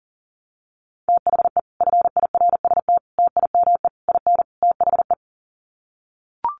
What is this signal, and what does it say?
Morse code, heard as a keyed tone, about 40 wpm, 700 Hz. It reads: THE FIRST TIME IN THE
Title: Sets of 5 Words Encore 40wpm